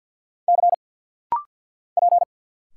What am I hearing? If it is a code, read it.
Morse code, heard as a keyed tone, about 50 wpm, 690 Hz. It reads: C P
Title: Single Letters - Rapid-Fire 50wpm